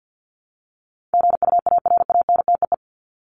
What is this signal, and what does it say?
Morse code, heard as a keyed tone, about 50 wpm, 700 Hz. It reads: GUARANTEE
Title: Top 1501-1750 Words 50wpm